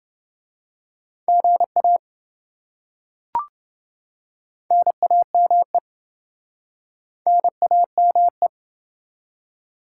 Morse code, heard as a keyed tone, about 28 wpm, 700 Hz. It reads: GA NAME NAME